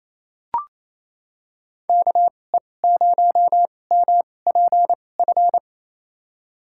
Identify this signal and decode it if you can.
Morse code, heard as a keyed tone, about 28 wpm, spaced as for 20 wpm, 700 Hz. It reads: KE0MPF
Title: USA Call Signs - 6 Characters - 2x Character Spacing - 28wpm